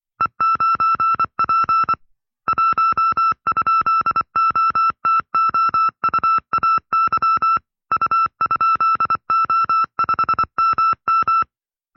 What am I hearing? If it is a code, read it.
Morse code, heard as a keyed tone, about 24 wpm, 1310 Hz. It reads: E9P 1?OTOUAY U?O5MM
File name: FreeFiles / CW-practice / 20WPM-side2